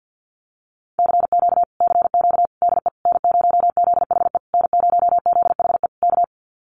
Morse code, heard as a keyed tone, about 50 wpm, 700 Hz. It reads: CQ CQ DE N0ZHE N0ZHE K
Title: Common QSO Phrases Encore 50wpm